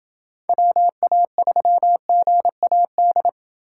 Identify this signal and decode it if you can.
Morse code, heard as a keyed tone, about 28 wpm, 700 Hz. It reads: WA3GAD